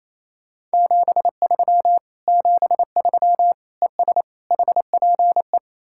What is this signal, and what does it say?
Morse code, heard as a keyed tone, about 35 wpm, 700 Hz. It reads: 73 73 ES HPE